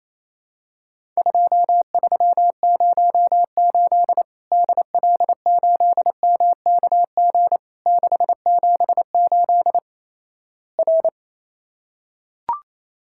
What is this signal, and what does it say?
Morse code, heard as a keyed tone, about 28 wpm, 700 Hz. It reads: 2308 DL8MKG 678 R
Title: Worked All Europe DX Contest - QTC Exchange - 28wpm